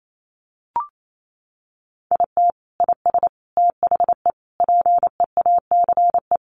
Morse code, heard as a keyed tone, about 28 wpm, 700 Hz. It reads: IT IS THE PEACE
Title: Sentences from Top 1500 Words 28wpm